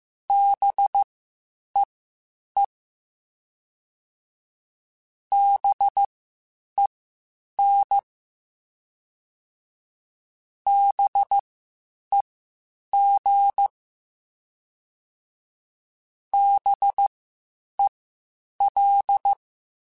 Morse code, heard as a keyed tone, about 15 wpm, spaced as for 8 wpm, 800 Hz. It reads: BEE BEN BEG BEL